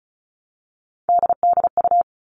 Morse code, heard as a keyed tone, about 34 wpm, 700 Hz. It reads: DDU
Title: All Cut Numbers - 3-Digit 35wpm